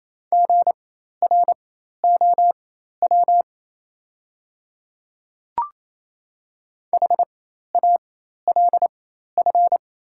Morse code, heard as a keyed tone, about 28 wpm, spaced as for 13 wpm, 690 Hz. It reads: GROW HALF
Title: 4x Character Spacing - Top 500 Words - 28wpm